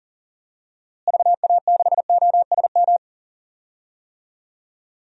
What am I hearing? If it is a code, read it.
Morse code, heard as a keyed tone, about 38 wpm, 700 Hz. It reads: VA6OSM
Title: DXCC-100 Call Signs - 6 Characters - Repeated 2x - 40wpm